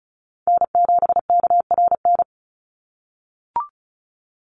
Morse code, heard as a keyed tone, about 34 wpm, 700 Hz. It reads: N7KRN
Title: NA SOTA Chaser Callsigns 35wpm